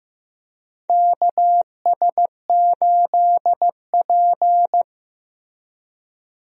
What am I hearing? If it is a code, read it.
Morse code, heard as a keyed tone, about 15 wpm, 700 Hz. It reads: KS8P
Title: USA Call Signs - 4 Characters - Repeated 3x - 15wpm